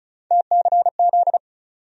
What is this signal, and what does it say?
Morse code, heard as a keyed tone, about 34 wpm, 700 Hz. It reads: TCZ
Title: USA Call Sign Postfixes - 3 Letters 35wpm